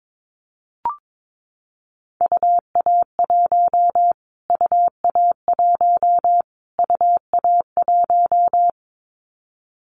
Morse code, heard as a keyed tone, about 22 wpm, 700 Hz. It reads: UA1 UA1 UA1
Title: Callsign Prefixes for Top 25 DXCC - Repeated 3x - 22wpm